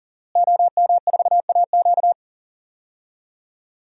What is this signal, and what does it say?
Morse code, heard as a keyed tone, about 40 wpm, 700 Hz. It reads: OM4AQ
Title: DXCC-100 Call Signs - 5 Characters - Repeated 3x - 40wpm